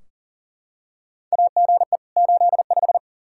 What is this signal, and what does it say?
Morse code, heard as a keyed tone, about 40 wpm, 700 Hz. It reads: AGE 85
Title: Practice QSO - Exchange 3 - More Info - 40wpm